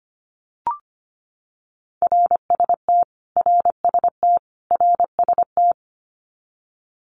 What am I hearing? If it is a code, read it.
Morse code, heard as a keyed tone, about 25 wpm, 700 Hz. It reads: RST RST RST
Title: Abbreviations Repeated 3x - 25wpm